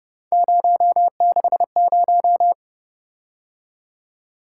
Morse code, heard as a keyed tone, about 29 wpm, 700 Hz. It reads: 060